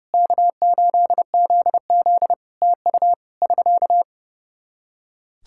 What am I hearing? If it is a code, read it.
Morse code, heard as a keyed tone, about 30 wpm, 700 Hz. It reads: K8ZZ TU <SK>
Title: Practice QSO - Exchange 4 - Ending - 30wpm